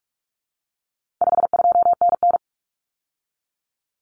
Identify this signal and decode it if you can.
Morse code, heard as a keyed tone, about 44 wpm, 700 Hz. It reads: 52NN
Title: Common Cut Numbers - 4-Digit 45wpm